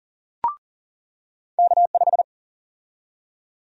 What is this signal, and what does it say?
Morse code, heard as a keyed tone, about 40 wpm, 700 Hz. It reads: K5
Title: USA Call Sign Prefixes - 40wpm